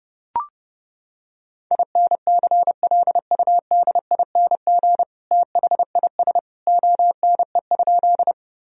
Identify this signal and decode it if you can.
Morse code, heard as a keyed tone, about 30 wpm, 700 Hz. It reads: INCLUDING THIS ONE?
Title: Sentences from Top 1000 Words - Review Part 4 - 30wpm